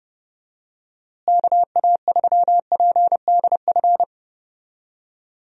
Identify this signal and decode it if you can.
Morse code, heard as a keyed tone, about 30 wpm, 700 Hz. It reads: KA3PDF
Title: USA Call Signs - 6 Characters - Repeated 3x - 30wpm